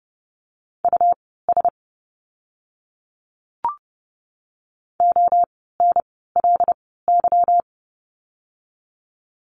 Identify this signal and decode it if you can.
Morse code, heard as a keyed tone, about 30 wpm, spaced as for 17 wpm, 700 Hz. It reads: US ONLY
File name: course / 30:10wpm / 230 Lesson 37f - Letter x - Words 30wpm